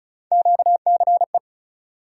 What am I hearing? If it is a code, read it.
Morse code, heard as a keyed tone, about 36 wpm, 700 Hz. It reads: QCE
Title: USA Call Sign Postfixes - 3 Letters 35wpm